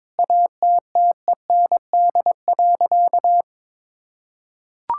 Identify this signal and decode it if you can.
Morse code, heard as a keyed tone, about 22 wpm, 700 Hz. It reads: ATTEND.
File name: Sentences with 7-Word Phrases - 22wpm